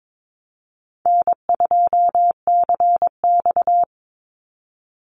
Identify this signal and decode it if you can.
Morse code, heard as a keyed tone, about 22 wpm, 700 Hz. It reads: N2CX
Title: Easy Call Signs - 22wpm